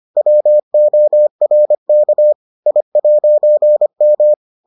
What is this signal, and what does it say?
Morse code, heard as a keyed tone, about 25 wpm, 600 Hz. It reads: WORK I'M